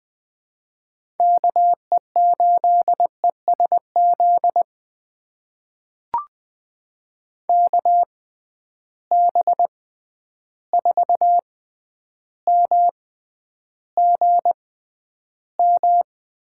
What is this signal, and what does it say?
Morse code, heard as a keyed tone, about 20 wpm, spaced as for 10 wpm, 700 Hz. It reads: KE8ESZ KB4MGM
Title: USA Call Signs - 6 Characters - 6x Character Spacing - 20wpm